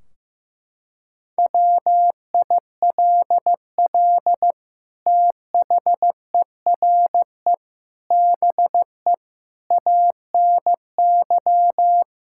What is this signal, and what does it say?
Morse code, heard as a keyed tone, about 15 wpm, 700 Hz. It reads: WILL THERE BE ANY
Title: Sentences from Top 1600 Words 15wpm